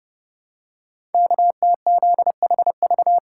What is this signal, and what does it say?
Morse code, heard as a keyed tone, about 30 wpm, 700 Hz. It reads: KTZHV
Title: Code Groups - 5 Letters - 30wpm